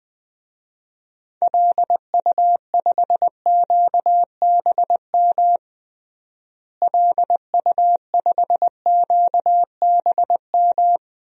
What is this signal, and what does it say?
Morse code, heard as a keyed tone, about 21 wpm, 700 Hz. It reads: LU5QBM LU5QBM
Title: DXCC-100 Call Signs - 6 Characters - Repeated 2x - 20wpm